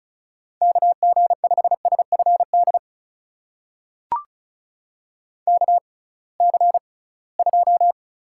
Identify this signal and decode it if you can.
Morse code, heard as a keyed tone, about 36 wpm, spaced as for 23 wpm, 700 Hz. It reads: KG5SFD KC2
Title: USA Call Signs - 6 Characters - 6x Character Spacing - 35wpm